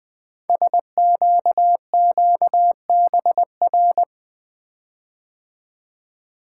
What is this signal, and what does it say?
Morse code, heard as a keyed tone, about 20 wpm, 700 Hz. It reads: SQQBR
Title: Code Groups - 5 Letters - 20wpm